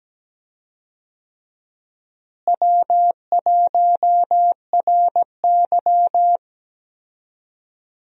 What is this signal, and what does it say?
Morse code, heard as a keyed tone, about 17 wpm, 700 Hz. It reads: W1RY
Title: USA Call Signs - 4 Characters - Repeated 3x - 17wpm